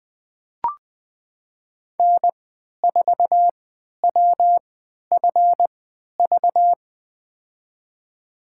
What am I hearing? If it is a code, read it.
Morse code, heard as a keyed tone, about 20 wpm, spaced as for 11 wpm, 700 Hz. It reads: N4WFV
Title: USA Call Signs - 5 Characters - 3x Character Spacing - 20wpm